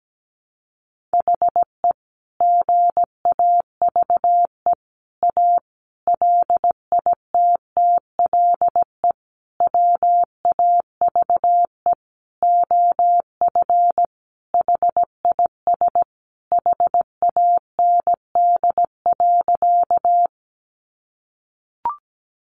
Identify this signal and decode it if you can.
Morse code, heard as a keyed tone, about 17 wpm, 700 Hz. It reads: HE GAVE A LITTLE WAVE OF HIS HAND.
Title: Sentences from Top 1400 Words 17wpm